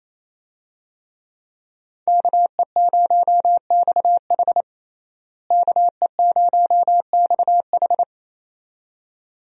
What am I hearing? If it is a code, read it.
Morse code, heard as a keyed tone, about 28 wpm, 700 Hz. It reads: KE0XH KE0XH
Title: USA Call Signs - 5 Characters - Repeated 2x - 28wpm